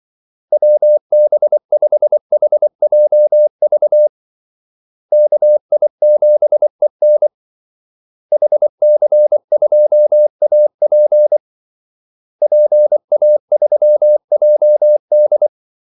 Morse code, heard as a keyed tone, about 24 wpm, 600 Hz. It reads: WB5HJV KI7EN HC2AP PA3JD
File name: Callsigns-24wpm